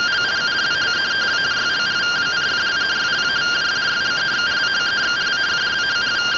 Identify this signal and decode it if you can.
Two-tone teleprinter signal, about 170 Hz shift.